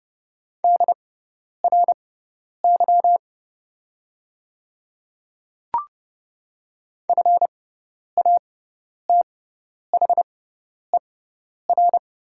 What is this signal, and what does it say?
Morse code, heard as a keyed tone, about 30 wpm, spaced as for 10 wpm, 700 Hz. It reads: DRY FATHER
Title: 6x Character Spacing - Top 500 Words - 30wpm